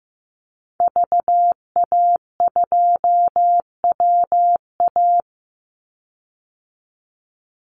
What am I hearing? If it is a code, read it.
Morse code, heard as a keyed tone, about 15 wpm, 700 Hz. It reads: VA2WA
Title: Call Signs Repeated 2x - 15wpm